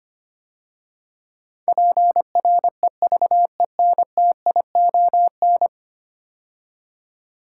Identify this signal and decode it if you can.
Morse code, heard as a keyed tone, about 25 wpm, 700 Hz. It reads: PREVENTION
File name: 10 Letter Words - First 3 Letters Given - 25wpm